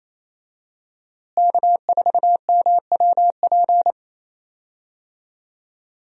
Morse code, heard as a keyed tone, about 28 wpm, 700 Hz. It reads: K4MWP
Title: USA Call Signs - 5 Characters - Repeated 3x - 28wpm